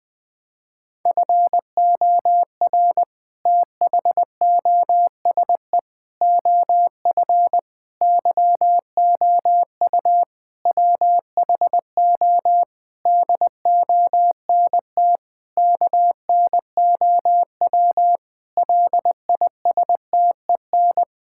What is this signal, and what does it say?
Morse code, heard as a keyed tone, about 20 wpm, 700 Hz. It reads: FOR THOSE OF YOU WHO DONT KNOW LISTEN
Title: Sentences with 7-Word Phrases - 20wpm